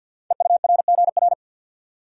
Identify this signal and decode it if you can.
Morse code, heard as a keyed tone, about 51 wpm, 700 Hz. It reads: EURGR
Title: Code Groups - 5 Letters - 50wpm